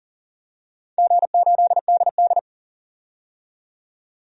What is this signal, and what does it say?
Morse code, heard as a keyed tone, about 38 wpm, 700 Hz. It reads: G8DD
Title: DXCC-100 Call Signs - 4 Characters - 40wpm